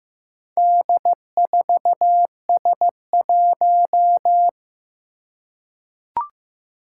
Morse code, heard as a keyed tone, about 15 wpm, 700 Hz. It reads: D4S1
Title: Four Challenging Letter-Number Combinations 15wpm